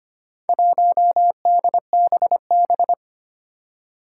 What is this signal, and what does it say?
Morse code, heard as a keyed tone, about 25 wpm, 700 Hz. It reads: 1DBB
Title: Four Challenging Letter-Number Combinations 25wpm